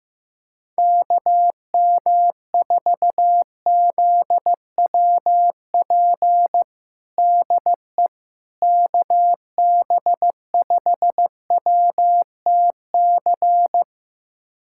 Morse code, heard as a keyed tone, about 15 wpm, 700 Hz. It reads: KM4ZWP DE KB5WTC
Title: USA Callsign DE Callsign 15wpm